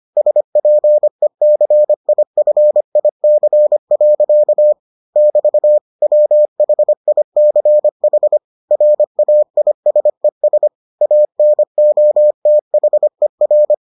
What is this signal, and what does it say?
Morse code, heard as a keyed tone, about 25 wpm, 600 Hz. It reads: SPECIFIC. <BT> WHICH RAISES ANOTHER